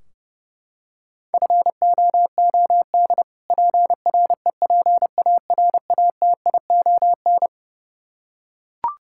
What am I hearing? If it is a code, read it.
Morse code, heard as a keyed tone, about 29 wpm, 700 Hz. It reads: FOOD PREPARATION
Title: Practice QSO - Exchange 3 - More Info - 30wpm